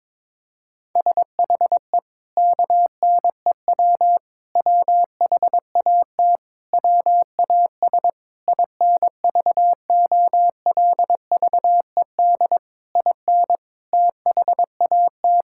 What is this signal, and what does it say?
Morse code, heard as a keyed tone, about 22 wpm, 700 Hz. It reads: SHE KNEW WHAT WAS INVOLVED IN THAT
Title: Sentences from Top 1000 Words - Review Part 5 - 22wpm